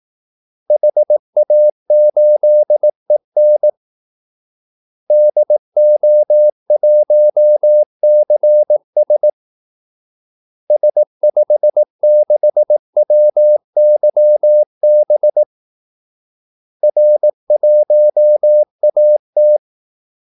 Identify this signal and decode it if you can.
Morse code, heard as a keyed tone, about 18 wpm, 600 Hz. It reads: HA8EN DO1CS S56WYB R1AT
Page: Kaldesignaler 18 wpm